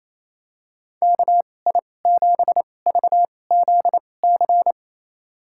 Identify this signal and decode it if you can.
Morse code, heard as a keyed tone, about 28 wpm, spaced as for 20 wpm, 700 Hz. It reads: KI7VZC
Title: USA Call Signs - 6 Characters - 2x Character Spacing - 28wpm